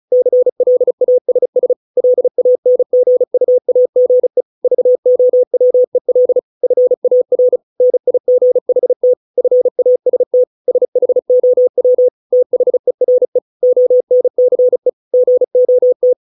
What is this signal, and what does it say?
Morse code, heard as a keyed tone, about 34 wpm, 500 Hz. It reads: CLASS LANGUAGE VOWEL FAR NIGHT FAST SHOW THERE ONCE GOT